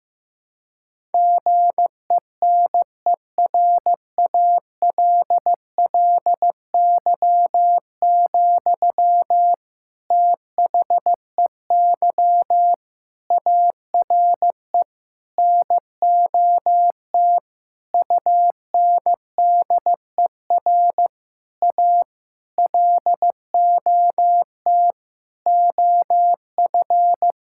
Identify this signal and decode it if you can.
Morse code, heard as a keyed tone, about 15 wpm, 700 Hz. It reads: GENERALLY, THEY ARE NOT UNDER A LOT OF
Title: Sentences from Top 1000 Words - Review Part 6 - 15wpm